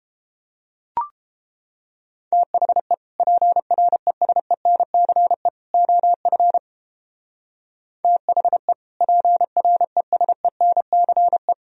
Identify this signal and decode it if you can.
Morse code, heard as a keyed tone, about 29 wpm, 700 Hz. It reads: THE PRESENCE OF THE PRESENCE